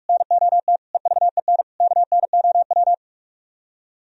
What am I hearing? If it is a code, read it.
Morse code, heard as a keyed tone, about 44 wpm, 700 Hz. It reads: NOT EVEN KNOW